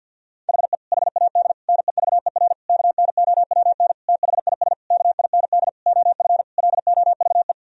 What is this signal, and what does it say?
Morse code, heard as a keyed tone, about 53 wpm, 700 Hz. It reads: HE HAD NEVER KNOWN THIS KIND OF LOVE
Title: Sentences from Top 1000 Words - Review Part 3 - 50wpm